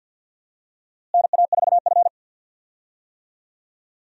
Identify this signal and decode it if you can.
Morse code, heard as a keyed tone, about 52 wpm, 700 Hz. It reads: NA4F
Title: USA Call Signs - 4 Characters - Repeated 2x - 50wpm